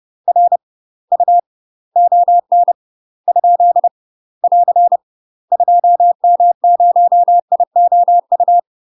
Morse code, heard as a keyed tone, about 30 wpm, 700 Hz. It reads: R U ON ? <AR> 2M0IOU
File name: UK MASTER QSOs TRAINING/30WPM